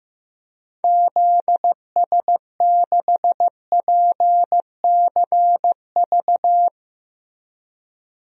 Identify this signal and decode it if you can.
Morse code, heard as a keyed tone, about 15 wpm, 700 Hz. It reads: ZS6PCV
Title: DXCC-100 Call Signs - 6 Characters - Repeated 2x - 15wpm